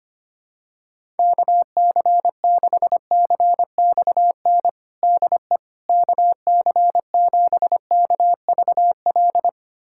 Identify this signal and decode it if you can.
Morse code, heard as a keyed tone, about 25 wpm, 700 Hz. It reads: KC6CXN DE KC7KVL
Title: USA Callsign DE Callsign - 6 Characters - 25wpm